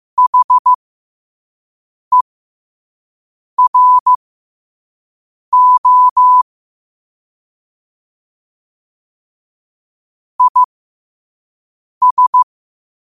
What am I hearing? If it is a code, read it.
Morse code, heard as a keyed tone, about 15 wpm, spaced as for 5 wpm, 1000 Hz. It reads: HERO IS
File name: qotdfiles / 5wpm